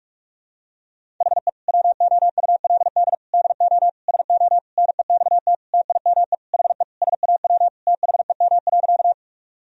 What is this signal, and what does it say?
Morse code, heard as a keyed tone, about 46 wpm, 700 Hz. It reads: HE WOULD DO SO NEXT TIME HE SAW THEM.
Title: Sentences from Top 1000 Words - Review Part 1 - 45wpm